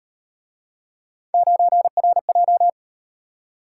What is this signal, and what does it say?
Morse code, heard as a keyed tone, about 37 wpm, 700 Hz. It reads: 9RJ